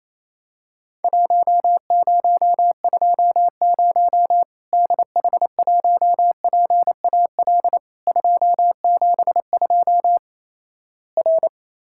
Morse code, heard as a keyed tone, about 28 wpm, 700 Hz. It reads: 1020 DH1PAL 272 R
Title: Worked All Europe DX Contest - QTC Exchange - 28wpm